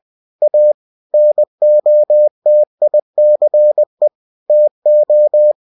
Morse code, heard as a keyed tone, about 20 wpm, 600 Hz. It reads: A NOTICE TO
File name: cw-books / winnie-the-pooh / winnie-20wpm-page-0030